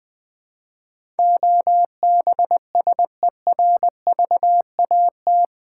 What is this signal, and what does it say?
Morse code, heard as a keyed tone, about 20 wpm, 700 Hz. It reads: OBSERVAT
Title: Build A Word - 4 Letters At A Time - 11-Letter Words - 20wpm